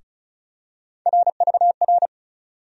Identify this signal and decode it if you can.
Morse code, heard as a keyed tone, about 36 wpm, 700 Hz. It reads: RVR
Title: USA Call Sign Postfixes - 3 Letters 35wpm